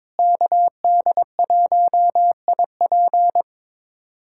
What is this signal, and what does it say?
Morse code, heard as a keyed tone, about 22 wpm, 700 Hz. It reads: KD1IP
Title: USA Callsign DE Callsign - 5 Characters - 22wpm